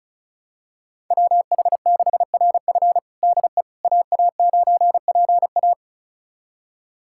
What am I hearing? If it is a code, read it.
Morse code, heard as a keyed tone, about 36 wpm, 700 Hz. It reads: WH6RF DE AA9PA
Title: USA Callsign DE Callsign - 5 Characters - 35wpm